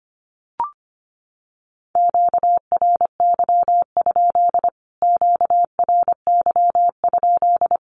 Morse code, heard as a keyed tone, about 25 wpm, 700 Hz. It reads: QRY? QRY?
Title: Q-Codes Repeated 3x 25wpm